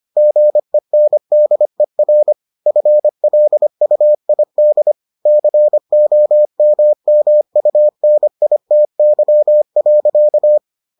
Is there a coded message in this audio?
Yes: Morse code from 0.2 to 10.6 s, heard as a keyed tone, about 25 wpm, 600 Hz. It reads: GENDER FLUID COMMUNITY.